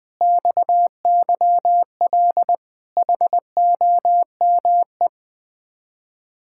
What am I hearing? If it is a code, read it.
Morse code, heard as a keyed tone, about 20 wpm, 700 Hz. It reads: XYL HOME
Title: Speed-Racing Common QSO Phrases - 17wpm